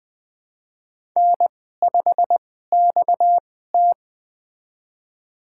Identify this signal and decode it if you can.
Morse code, heard as a keyed tone, about 20 wpm, spaced as for 15 wpm, 700 Hz. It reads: N5XT
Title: USA Call Signs - 4 Characters - 2x Character Spacing - 20wpm